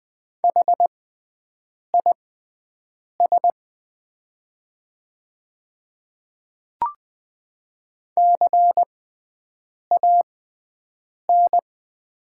Morse code, heard as a keyed tone, about 20 wpm, spaced as for 7 wpm, 700 Hz. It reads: HIS CAN
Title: 6x Character Spacing - Top 100 Words - 20wpm